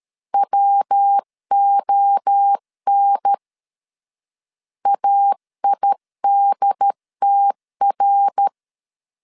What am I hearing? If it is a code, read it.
Morse code, heard as a keyed tone, about 13 wpm, 800 Hz. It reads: WON AIDTR
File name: CW / DET / 1 3 1500C S A 12wpm 2905